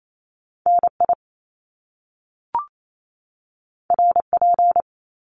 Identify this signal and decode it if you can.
Morse code, heard as a keyed tone, about 28 wpm, 700 Hz. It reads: NI RP
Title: Top Two Letter Combinations in Words 28wpm